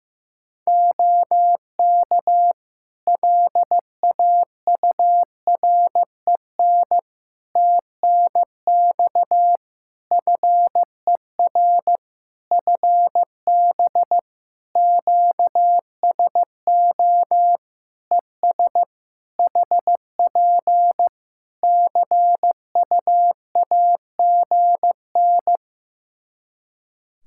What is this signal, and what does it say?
Morse code, heard as a keyed tone, about 15 wpm, 700 Hz. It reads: OK LAUREN TNX FER FB QSO ES HP CUAGN
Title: Practice QSO - Exchange 4 - Ending - 15wpm